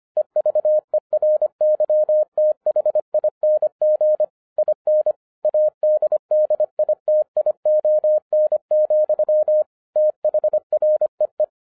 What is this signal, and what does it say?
Morse code, heard as a keyed tone, about 25 wpm, 610 Hz. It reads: EVERYTHING IN ADDITION, THREE
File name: 2011-07-mp3 / n24 QRP OTAKU 25wpm